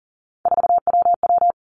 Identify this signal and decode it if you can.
Morse code, heard as a keyed tone, about 40 wpm, 700 Hz. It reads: 4WW
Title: Three Challenging Letter-Number Combinations 40wpm